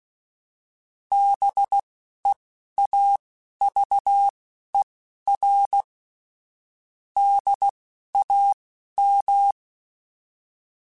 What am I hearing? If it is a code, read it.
Morse code, heard as a keyed tone, about 16 wpm, spaced as for 11 wpm, 800 Hz. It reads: BEAVER DAM
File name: fromSMs / FromSMs mp3s / Cities@10